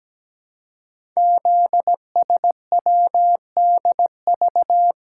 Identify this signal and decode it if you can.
Morse code, heard as a keyed tone, about 17 wpm, 700 Hz. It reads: ZSWDV
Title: Code Groups - 5 Letters - 17wpm